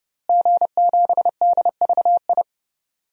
Morse code, heard as a keyed tone, about 30 wpm, 700 Hz. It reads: G7DVI
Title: DXCC-100 Call Signs - 5 Characters - Repeated 3x - 30wpm